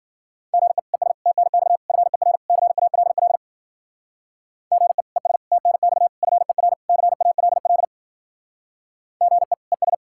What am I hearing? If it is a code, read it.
Morse code, heard as a keyed tone, about 49 wpm, 700 Hz. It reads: GE ES TNX FER CALL GE ES TNX FER CALL GE ES
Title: Speed-Racing Common QSO Phrases - 40wpm